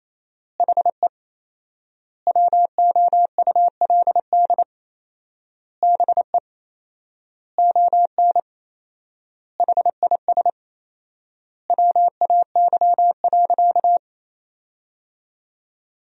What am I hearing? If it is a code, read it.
Morse code, heard as a keyed tone, about 28 wpm, 700 Hz. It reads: HE WOULD BE ON HIS WAY.
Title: 4x Word-Spacing - Sentences from Top 300 Words - 28wpm